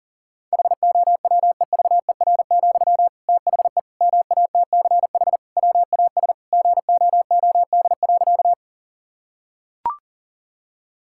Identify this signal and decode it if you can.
Morse code, heard as a keyed tone, about 37 wpm, 700 Hz. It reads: HOWEVER, THE MATCH WAS GOOD.
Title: Sentences from Top 1600 Words 40wpm